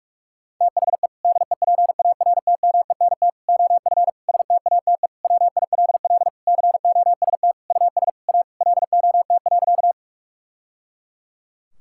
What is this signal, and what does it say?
Morse code, heard as a keyed tone, about 41 wpm, 700 Hz. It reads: THE DEPARTMENT OF STATE WILL COST US A LOT.
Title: Sentences from Top 1000 Words - Review Part 5 - 45wpm